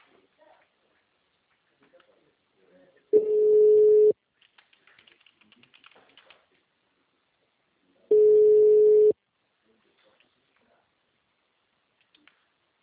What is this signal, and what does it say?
Ringback tone (Europe, 425 Hz), with speech.